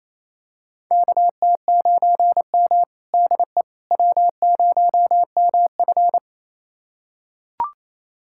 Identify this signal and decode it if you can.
Morse code, heard as a keyed tone, about 28 wpm, 700 Hz. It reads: KT9M DE W0MF
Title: USA Callsign DE Callsign - 4 Characters - 28wpm